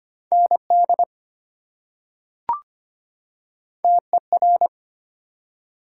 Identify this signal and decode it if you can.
Morse code, heard as a keyed tone, about 25 wpm, 700 Hz. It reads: ND TER
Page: morse-code-ninja-practice-25wpm